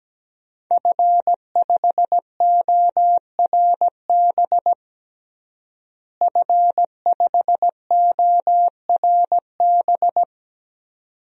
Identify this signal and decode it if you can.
Morse code, heard as a keyed tone, about 17 wpm, 700 Hz. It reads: F5ORB F5ORB
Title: DXCC-100 Call Signs - 5 Characters - Repeated 2x - 17wpm